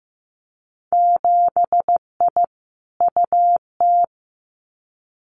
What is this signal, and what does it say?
Morse code, heard as a keyed tone, about 15 wpm, 700 Hz. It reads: 7I UT
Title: Winter Field Day Exchanges Encore 15wpm